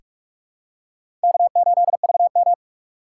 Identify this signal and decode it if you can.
Morse code, heard as a keyed tone, about 42 wpm, 700 Hz. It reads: K8VM
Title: USA Call Signs - 2x Character Spacing - 45wpm